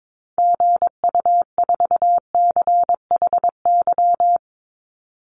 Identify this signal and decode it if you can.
Morse code, heard as a keyed tone, about 22 wpm, 700 Hz. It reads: GU4CHY
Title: Call Signs Repeated 3x - 22wpm